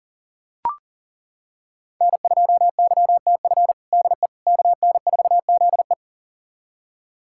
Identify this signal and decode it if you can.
Morse code, heard as a keyed tone, about 37 wpm, 700 Hz. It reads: N2YTF DE KN4ZE
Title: USA Callsign DE Callsign - 5 Characters - 40wpm